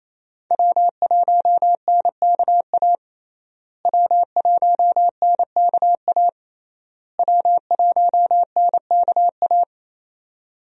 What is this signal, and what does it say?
Morse code, heard as a keyed tone, about 28 wpm, 700 Hz. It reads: W1NKA W1NKA W1NKA
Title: USA Call Signs - 5 Characters - Repeated 3x - 28wpm